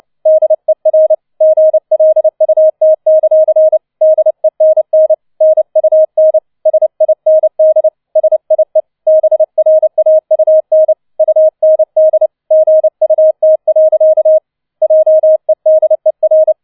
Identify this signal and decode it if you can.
Morse code, heard as a keyed tone, about 29 wpm, 630 Hz. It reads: DER GLUT; DENN NUN SIND SIE BRAUN UND GUT. JEDER
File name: 006 Sechster Streich 30wpm